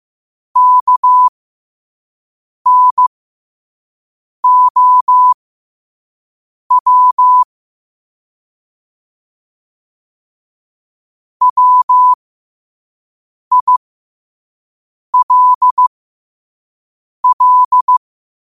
Morse code, heard as a keyed tone, about 15 wpm, spaced as for 5 wpm, 1000 Hz. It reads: KNOW WILL
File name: qotdfiles / 5wpm